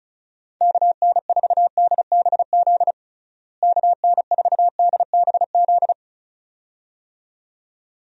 Morse code, heard as a keyed tone, about 36 wpm, 700 Hz. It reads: KN4DBZ KN4DBZ
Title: USA Call Signs - 6 Characters - Repeated 2x - 35wpm